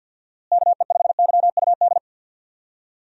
Morse code, heard as a keyed tone, about 51 wpm, 700 Hz. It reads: KE5YUD
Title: USA Call Signs - 6 Characters - Repeated 2x - 50wpm